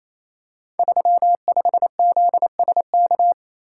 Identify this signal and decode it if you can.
Morse code, heard as a keyed tone, about 28 wpm, 700 Hz. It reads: 35ZSK
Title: Code Groups - 5 Letter:Numbers - 28wpm